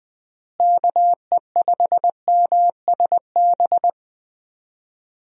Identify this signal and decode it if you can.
Morse code, heard as a keyed tone, about 20 wpm, 700 Hz. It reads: KE5MSB
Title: USA Call Signs - 6 Characters - Repeated 3x - 20wpm